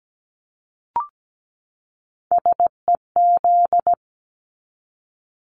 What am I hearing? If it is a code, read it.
Morse code, heard as a keyed tone, about 17 wpm, 700 Hz. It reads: SEZ
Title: Abbreviations 17wpm